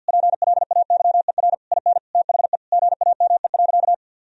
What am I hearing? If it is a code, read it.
Morse code, heard as a keyed tone, about 50 wpm, 700 Hz. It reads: PLAYER IN THE GAME.
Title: Sentences from Top 1000 Words - Review Part 4 - 50wpm